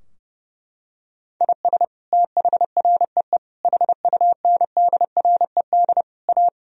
Morse code, heard as a keyed tone, about 32 wpm, 700 Hz. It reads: IS THREE HUNDRED A
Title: Sentences from Top 1000 Words - Review Part 2 - 30wpm